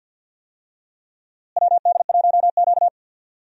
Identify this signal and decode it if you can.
Morse code, heard as a keyed tone, about 47 wpm, 700 Hz. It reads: WD1Q
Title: USA Call Signs - 4 Characters - Repeated 2x - 50wpm